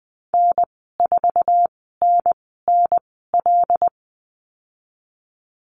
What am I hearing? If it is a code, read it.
Morse code, heard as a keyed tone, about 19 wpm, spaced as for 14 wpm, 700 Hz. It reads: N4NNL
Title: 076 Lesson 13g - Letter L - Call Signs 20wpm